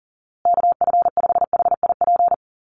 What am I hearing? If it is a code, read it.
Morse code, heard as a keyed tone, about 37 wpm, 700 Hz. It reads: KF5HIP
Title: USA Call Signs Repeated 3x 40wpm